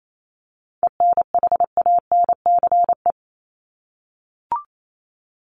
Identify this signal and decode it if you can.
Morse code, heard as a keyed tone, about 28 wpm, 700 Hz. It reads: ENHANCE
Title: Top 1251-1500 Words 28wpm